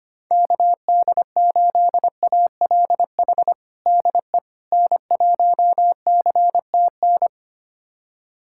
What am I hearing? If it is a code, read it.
Morse code, heard as a keyed tone, about 25 wpm, 700 Hz. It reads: KD8ALH DE N1CTN
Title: USA Callsign DE Callsign 25wpm